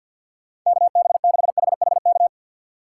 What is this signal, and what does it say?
Morse code, heard as a keyed tone, about 53 wpm, 700 Hz. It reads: KB6HHK
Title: USA Call Signs - 6 Characters - 6x Character Spacing - 50wpm